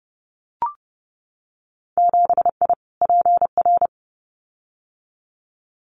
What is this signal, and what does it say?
Morse code, heard as a keyed tone, about 30 wpm, 700 Hz. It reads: 7I PR
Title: Winter Field Day Exchanges Encore 30wpm